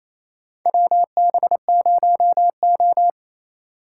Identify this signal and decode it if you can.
Morse code, heard as a keyed tone, about 28 wpm, 700 Hz. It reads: WB0O
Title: USA Call Signs - 8x Character Spacing - 28wpm